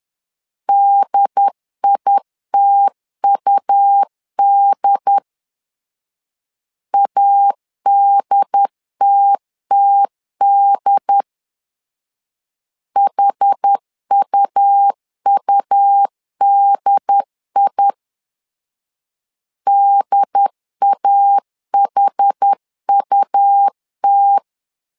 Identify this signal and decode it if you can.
Morse code, heard as a keyed tone, about 10 wpm, 800 Hz. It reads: DITUD ADTTD HUUDI DAHUT
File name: LEZIONI / 2 UHDN / 2 UHD 10wpm 1826